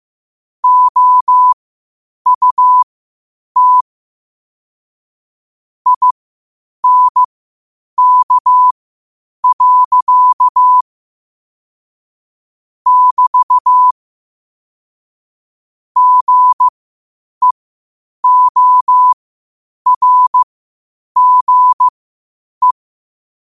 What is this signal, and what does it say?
Morse code, heard as a keyed tone, about 15 wpm, spaced as for 8 wpm, 1000 Hz. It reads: OUT INK. <BT> GEORGE